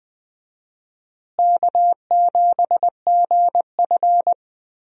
Morse code, heard as a keyed tone, about 20 wpm, 700 Hz. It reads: K7GF
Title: USA Call Signs - 4 Characters - Repeated 3x - 20wpm